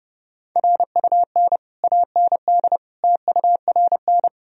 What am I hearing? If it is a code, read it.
Morse code, heard as a keyed tone, about 30 wpm, 700 Hz. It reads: RUN AND TURN
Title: Sentences from Top 1000 Words - Review Part 3 - 30wpm